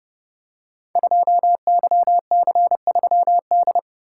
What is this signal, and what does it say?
Morse code, heard as a keyed tone, about 30 wpm, 700 Hz. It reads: 2YC3D
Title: Code Groups - 5 Letter:Numbers - 30wpm